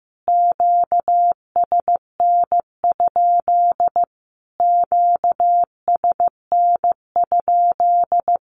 Morse code, heard as a keyed tone, about 15 wpm, 700 Hz. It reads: QSN? QSN?
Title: Q-Codes Repeated 3x 15wpm